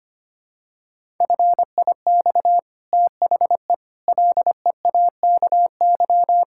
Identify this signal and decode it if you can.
Morse code, heard as a keyed tone, about 25 wpm, 700 Hz. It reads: FIX THE LEAKY
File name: Easy Sentences with Hints - 25wpm